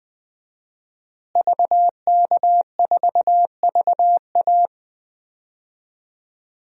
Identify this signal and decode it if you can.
Morse code, heard as a keyed tone, about 20 wpm, 700 Hz. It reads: VK4VA
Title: DXCC-100 Call Signs - 5 Characters - Repeated 3x - 20wpm